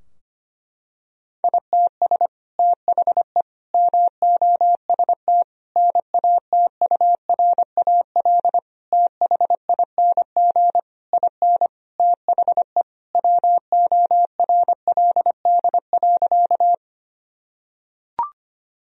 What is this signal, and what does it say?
Morse code, heard as a keyed tone, about 25 wpm, 700 Hz. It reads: ITS THE MOST NATURAL THING IN THE WORLD.
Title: Sentences from Top 1000 Words - Review Part 4 - 25wpm